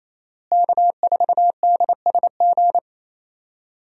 Morse code, heard as a keyed tone, about 28 wpm, 700 Hz. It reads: K4DSG
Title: USA Call Signs - 5 Characters - Repeated 2x - 28wpm